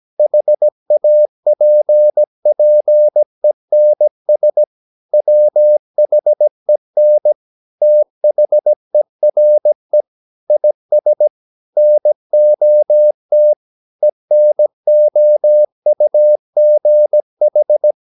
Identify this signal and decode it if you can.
Morse code, heard as a keyed tone, about 17 wpm, 600 Hz. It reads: HAPPENS WHEN THERE IS NOT ENOUGH
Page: Romanen Never af Ken Follet på engelsk.